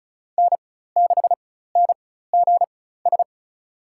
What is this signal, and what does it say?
Morse code, heard as a keyed tone, about 36 wpm, spaced as for 16 wpm, 700 Hz. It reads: N6NGS
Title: USA Call Signs - 5 Characters - 4x Character Spacing - 35wpm